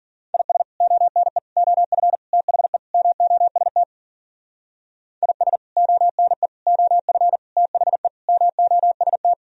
Morse code, heard as a keyed tone, about 50 wpm, 700 Hz. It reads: IS ONE OF THE MOST IS ONE OF THE MOST
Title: Speed-Racing - Sets of 5 Words 40wpm